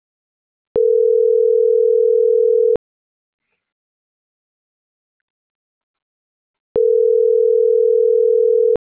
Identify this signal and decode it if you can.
Ringback tone (North America), 440 + 480 Hz.